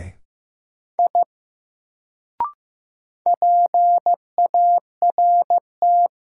Morse code, heard as a keyed tone, about 15 wpm, 700 Hz. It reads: I PART
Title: Sentences from Top 200 Words - New Word Drill - 15wpm